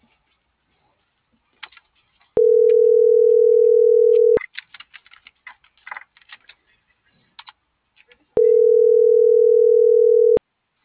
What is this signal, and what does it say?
Ringback tone (North America), 440 + 480 Hz.